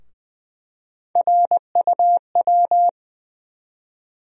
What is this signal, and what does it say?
Morse code, heard as a keyed tone, about 21 wpm, 700 Hz. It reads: RUW
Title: USA Call Sign Postfixes - 3 Letters 20wpm